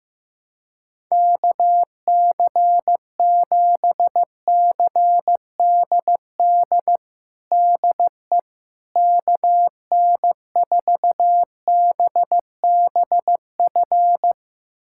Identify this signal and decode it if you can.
Morse code, heard as a keyed tone, about 15 wpm, 700 Hz. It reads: KC7CDD DE KN4BBF
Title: USA Callsign DE Callsign - 6 Characters - 15wpm